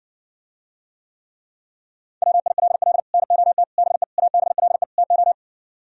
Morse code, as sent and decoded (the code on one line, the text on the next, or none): .-- .. .-.. .-.. / -. --- - / -... / .- -... .-.. / - ---
WILL NOT BE ABLE TO